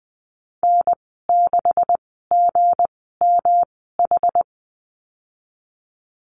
Morse code, heard as a keyed tone, about 20 wpm, spaced as for 14 wpm, 700 Hz. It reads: N6GMH
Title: 238 Lesson 38g - Letter q - Call Signs 20wpm